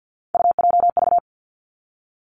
Morse code, heard as a keyed tone, about 50 wpm, 700 Hz. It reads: UJV
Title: Three Challenging Letter-Number Combinations 50wpm